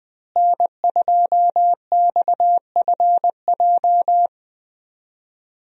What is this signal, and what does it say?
Morse code, heard as a keyed tone, about 20 wpm, 700 Hz. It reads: N2XFJ
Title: USA Call Signs - 5 Characters - Repeated 3x - 20wpm